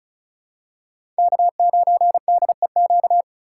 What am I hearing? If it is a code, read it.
Morse code, heard as a keyed tone, about 34 wpm, 700 Hz. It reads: K9DEQ
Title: USA Call Signs - 5 Characters - 4x Character Spacing - 35wpm